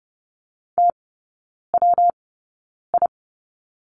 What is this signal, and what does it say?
Morse code, heard as a keyed tone, about 30 wpm, spaced as for 9 wpm, 700 Hz. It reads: TWI
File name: Lesson 19e - Letter w - Three Characters 30wpm